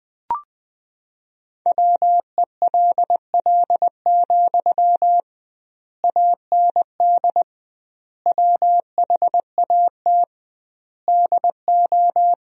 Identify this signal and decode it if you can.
Morse code, heard as a keyed tone, about 21 wpm, 700 Hz. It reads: WELL, AND WHAT DO
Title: 2x Word-Spacing - Sentences from Top 200 Words - 20wpm